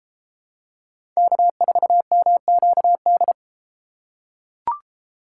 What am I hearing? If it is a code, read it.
Morse code, heard as a keyed tone, about 33 wpm, 700 Hz. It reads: K4MQD